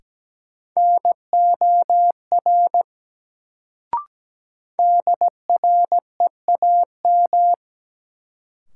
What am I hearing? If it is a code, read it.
Morse code, heard as a keyed tone, about 17 wpm, 700 Hz. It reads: NOR DREAM
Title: Sentences from Top 1500 Words - New Word Drill - 17wpm